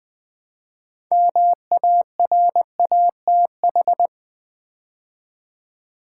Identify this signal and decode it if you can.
Morse code, heard as a keyed tone, about 20 wpm, 700 Hz. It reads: MARATH
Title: Build A Word - 3 Letters At A Time - 8-Letter Words - 20wpm